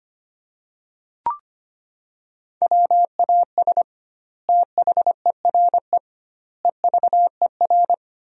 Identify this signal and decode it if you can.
Morse code, heard as a keyed tone, about 25 wpm, 700 Hz. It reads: WAS THERE EVER
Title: 2x Word-Spacing - Sentences from Top 400 Words - 25wpm